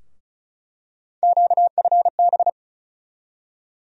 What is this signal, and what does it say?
Morse code, heard as a keyed tone, about 36 wpm, 700 Hz. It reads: QFB
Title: USA Call Sign Postfixes - 3 Letters 35wpm